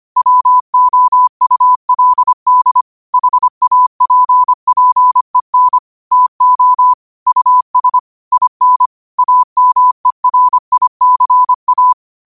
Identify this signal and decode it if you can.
Morse code, heard as a keyed tone, about 25 wpm, 1000 Hz. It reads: WOULD HAPPEN TO US IN AMERICA
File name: qotdfiles / 25wpm